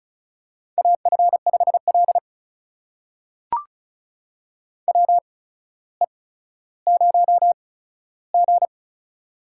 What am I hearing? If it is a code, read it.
Morse code, heard as a keyed tone, about 36 wpm, spaced as for 15 wpm, 700 Hz. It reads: AF5L WE0G
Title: USA Call Signs - 4 Characters - 8x Character Spacing - 35wpm